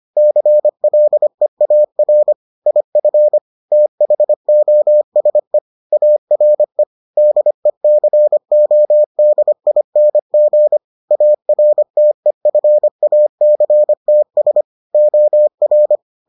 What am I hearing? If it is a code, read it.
Morse code, heard as a keyed tone, about 25 wpm, 600 Hz. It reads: CLEAR IF THOSE ARE DECODING ARTEFACTS OR